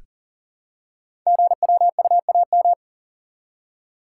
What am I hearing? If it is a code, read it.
Morse code, heard as a keyed tone, about 40 wpm, 700 Hz. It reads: GWUAM
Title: Code Groups - 5 Letters - 40wpm